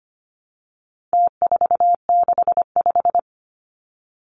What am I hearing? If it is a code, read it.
Morse code, heard as a keyed tone, about 25 wpm, 700 Hz. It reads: T465
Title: Numbers - Contest Style - 4-Digit 25wpm